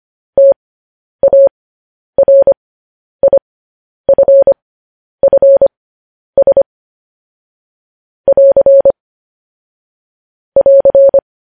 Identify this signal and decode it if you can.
Morse code, heard as a keyed tone, about 25 wpm, spaced as for 10 wpm, 550 Hz. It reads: TARIFFS <AR> <AR>
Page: news headlines in morse code at 10 wpm